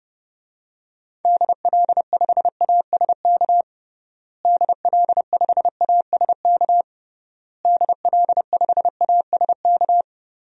Morse code, heard as a keyed tone, about 30 wpm, 700 Hz. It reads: DL5ASK DL5ASK DL5ASK
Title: DXCC-100 Call Signs - 6 Characters - Repeated 3x - 30wpm